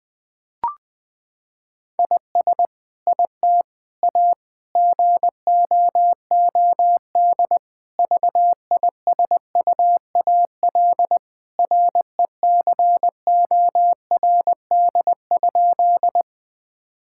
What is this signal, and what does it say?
Morse code, heard as a keyed tone, about 20 wpm, 700 Hz. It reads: IS IT A GOOD VISUAL RECORD?
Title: Sentences from Top 1000 Words - Review Part 6 - 20wpm